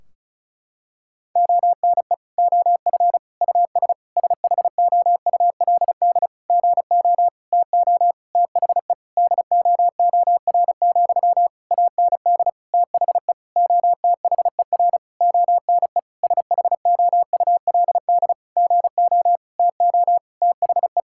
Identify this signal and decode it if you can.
Morse code, heard as a keyed tone, about 36 wpm, 700 Hz. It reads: ONE OF US SHOULD GO TO THE DOOR, AND THE OTHER ONE SHOULD GO TO THE
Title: Sentences from Top 1600 Words 35wpm